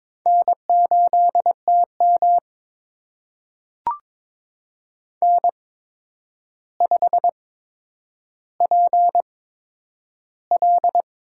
Morse code, heard as a keyed tone, about 22 wpm, spaced as for 9 wpm, 700 Hz. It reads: N8TM N5PL
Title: USA Call Signs - 8x Character Spacing - 22wpm